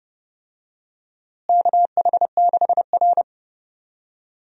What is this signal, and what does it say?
Morse code, heard as a keyed tone, about 30 wpm, 700 Hz. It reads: KH6R
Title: DXCC-100 Call Signs - 4 Characters - Repeated 2x - 30wpm